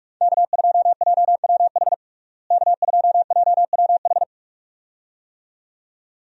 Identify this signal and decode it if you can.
Morse code, heard as a keyed tone, about 41 wpm, 700 Hz. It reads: K2JWH K2JWH
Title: USA Call Signs - 5 Characters - Repeated 2x - 45wpm